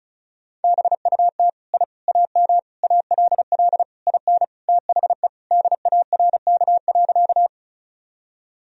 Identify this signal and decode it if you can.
Morse code, heard as a keyed tone, about 34 wpm, 700 Hz. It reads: BUT I AM ALL IN THE DARK.
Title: Sentences from Top 1000 Words - Review Part 2 - 35wpm